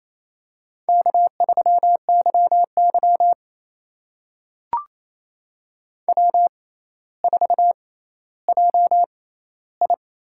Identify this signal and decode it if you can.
Morse code, heard as a keyed tone, about 28 wpm, spaced as for 14 wpm, 700 Hz. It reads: K3YY W4JI
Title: USA Call Signs - 4 Characters - 6x Character Spacing - 28wpm